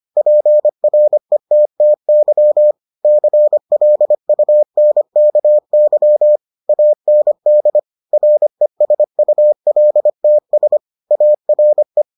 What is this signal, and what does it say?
Morse code, heard as a keyed tone, about 25 wpm, 600 Hz. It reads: PRETTY CLUNKY AND RESULTS ARE